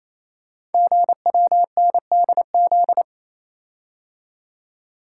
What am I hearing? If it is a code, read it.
Morse code, heard as a keyed tone, about 28 wpm, 700 Hz. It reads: GWNDZ
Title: Code Groups - 5 Letters - 28wpm